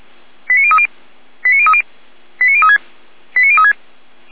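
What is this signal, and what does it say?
Five-tone selective call (ZVEI), heard as quick runs of five tones, tones of about 70 ms. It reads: 89030 89030 89036 89036
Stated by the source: ZVEI five-tone selective call 89084 received by SDR.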